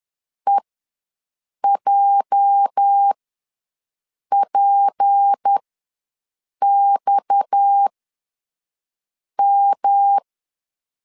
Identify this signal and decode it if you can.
Morse code, heard as a keyed tone, about 10 wpm, spaced as for 5 wpm, 800 Hz. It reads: EJPXM
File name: LEZIONI / 9 2867 ws 10wpm 4215